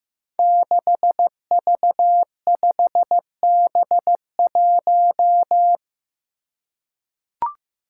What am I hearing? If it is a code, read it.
Morse code, heard as a keyed tone, about 15 wpm, 700 Hz. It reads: 6V5B1
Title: Five Challenging Letter-Number Combinations 15wpm